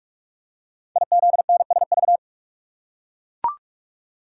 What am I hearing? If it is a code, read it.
Morse code, heard as a keyed tone, about 44 wpm, 700 Hz. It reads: IZNSV
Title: Code Groups - 5 Letters - 45wpm